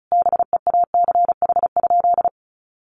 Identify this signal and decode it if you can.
Morse code, heard as a keyed tone, about 34 wpm, 700 Hz. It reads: BEACH?
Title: Sentences from Top 1500 Words 35wpm